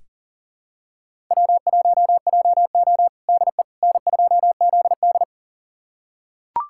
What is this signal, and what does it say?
Morse code, heard as a keyed tone, about 40 wpm, 700 Hz. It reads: W1JO DE N2ZD
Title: USA Callsign DE Callsign - 4 Characters - 40wpm